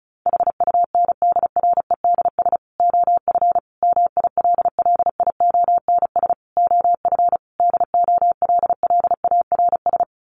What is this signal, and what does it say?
Morse code, heard as a keyed tone, about 34 wpm, 700 Hz. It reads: HUNDREDS OF MILLIONS OF DOLLARS
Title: Sets of 5 Words 35wpm